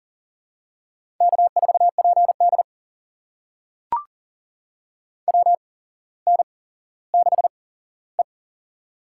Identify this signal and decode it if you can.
Morse code, heard as a keyed tone, about 37 wpm, spaced as for 17 wpm, 700 Hz. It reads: K4PD WN6E
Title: USA Call Signs - 4 Characters - 8x Character Spacing - 40wpm